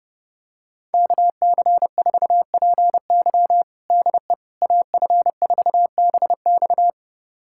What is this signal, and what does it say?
Morse code, heard as a keyed tone, about 31 wpm, 700 Hz. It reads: KC4PY DE AF4BX
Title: USA Callsign DE Callsign - 5 Characters - 30wpm